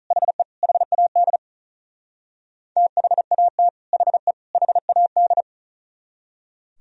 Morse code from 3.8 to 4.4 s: HE

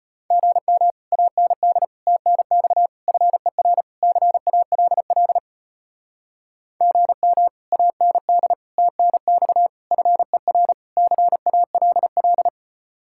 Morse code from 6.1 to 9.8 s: GM AND TNX

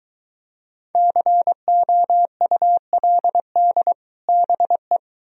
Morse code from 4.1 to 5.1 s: BE

